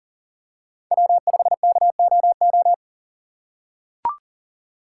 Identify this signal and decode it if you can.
Morse code, heard as a keyed tone, about 40 wpm, 700 Hz. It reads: W5KOO
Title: USA Call Signs - 6x Character Spacing - 40wpm